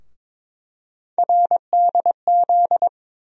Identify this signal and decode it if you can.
Morse code, heard as a keyed tone, about 22 wpm, 700 Hz. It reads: RDZ
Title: USA Call Sign Postfixes - 3 Letters 22wpm